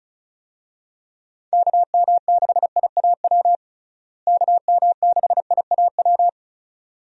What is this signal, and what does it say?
Morse code, heard as a keyed tone, about 36 wpm, 700 Hz. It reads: KM6IAW KM6IAW
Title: USA Call Signs - 6 Characters - Repeated 2x - 35wpm